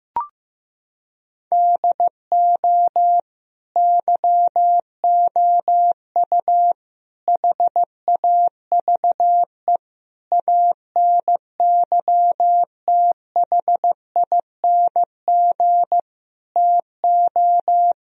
Morse code, heard as a keyed tone, about 15 wpm, 700 Hz. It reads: DO YOU HAVE ANYTHING TO
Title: Sentences from Top 1000 Words - Review Part 1 - 15wpm